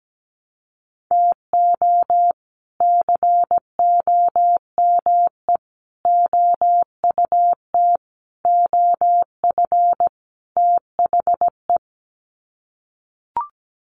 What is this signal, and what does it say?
Morse code, heard as a keyed tone, about 17 wpm, 700 Hz. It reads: TO COME OUT OF THE
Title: Sets of 5 Words Encore2 17wpm